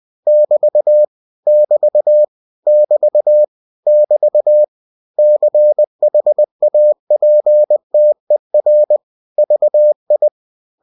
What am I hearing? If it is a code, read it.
Morse code, heard as a keyed tone, about 20 wpm, 600 Hz. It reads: <BT> <BT> <BT> <BT> CHAPTER VI
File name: cw-books / winnie-the-pooh / winnie-20wpm-page-0044